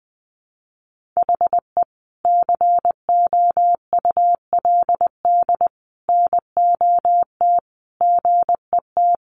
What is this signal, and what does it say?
Morse code, heard as a keyed tone, about 20 wpm, 700 Hz. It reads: HE COULD NOT GET
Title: Sentences from Top 1100 Words 20wpm